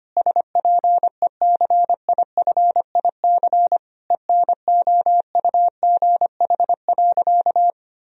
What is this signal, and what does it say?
Morse code, heard as a keyed tone, about 25 wpm, 700 Hz. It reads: SPECIFIC ENOUGH.
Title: Sentences from Top 1000 Words - Review Part 3 - 25wpm